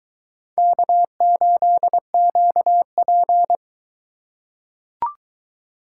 Morse code, heard as a keyed tone, about 21 wpm, 700 Hz. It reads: K8QP